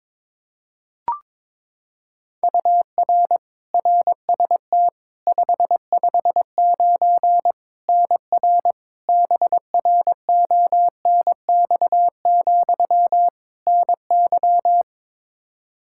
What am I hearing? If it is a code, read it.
Morse code, heard as a keyed tone, about 22 wpm, 700 Hz. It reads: UR RST 559 NR BRONX, NY
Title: K1USN SST QSO Practice 22wpm